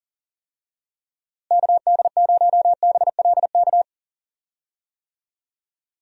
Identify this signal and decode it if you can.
Morse code, heard as a keyed tone, about 38 wpm, 700 Hz. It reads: KD0BLK
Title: USA Call Signs - 6 Characters - Repeated 3x - 40wpm